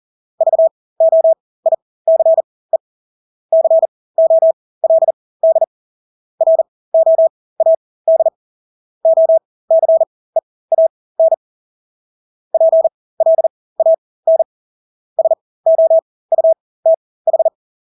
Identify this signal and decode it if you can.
Morse code, heard as a keyed tone, about 40 wpm, spaced as for 20 wpm, 650 Hz. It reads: VOICE COLD ROAD OCEAN PLAN SOUTH
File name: Engelske ord 40-20 wpm